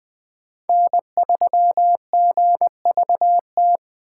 Morse code, heard as a keyed tone, about 20 wpm, 700 Hz. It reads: N3GVT
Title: USA Call Signs - 5 Characters - 6x Character Spacing - 20wpm